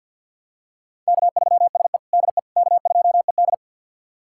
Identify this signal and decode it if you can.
Morse code, heard as a keyed tone, about 47 wpm, 700 Hz. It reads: K3SE DE K2ED
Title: USA Callsign DE Callsign - 4 Characters - 50wpm